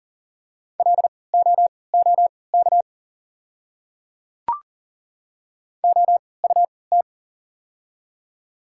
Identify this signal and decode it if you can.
Morse code, heard as a keyed tone, about 39 wpm, spaced as for 23 wpm, 690 Hz. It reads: LOOK OUT
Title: 3x Character Spacing - Top 100 Words - 40wpm